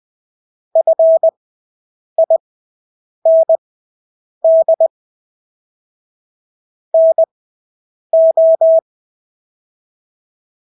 Morse code, heard as a keyed tone, about 20 wpm, spaced as for 8 wpm, 650 Hz. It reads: FIND NO
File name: Engelske ord 20/8 wpm